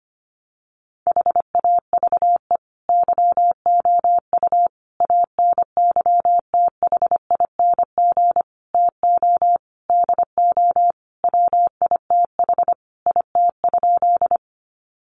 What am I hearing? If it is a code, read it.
Morse code, heard as a keyed tone, about 25 wpm, 700 Hz. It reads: HAVE YOU ANYTHING TO DO WITH IT?
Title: Sentences from Top 1000 Words 25wpm